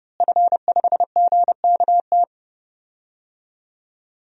Morse code, heard as a keyed tone, about 30 wpm, 700 Hz. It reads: F5GKT
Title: DXCC-100 Call Signs - 5 Characters - Repeated 3x - 30wpm